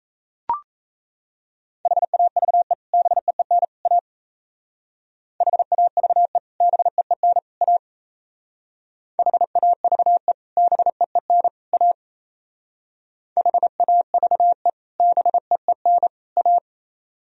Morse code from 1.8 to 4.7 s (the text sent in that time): HAVE BEEN A